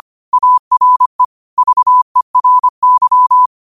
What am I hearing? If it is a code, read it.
Morse code, heard as a keyed tone, about 25 wpm, 1000 Hz. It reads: ARE VERY
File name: qotdfiles / 25wpm